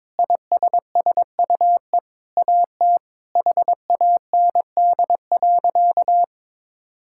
Morse code, heard as a keyed tone, about 22 wpm, 700 Hz. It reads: ISSUE AT HAND.
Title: Sentences from Top 1000 Words - Review Part 6 - 22wpm